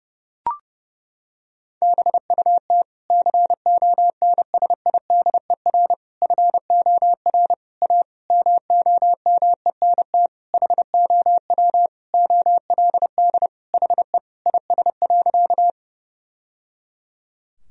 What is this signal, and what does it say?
Morse code, heard as a keyed tone, about 30 wpm, 700 Hz. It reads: BUT CONSIDER FOR A MOMENT HOW OLD HE IS.
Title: Sentences from Top 1000 Words - Review Part 4 - 30wpm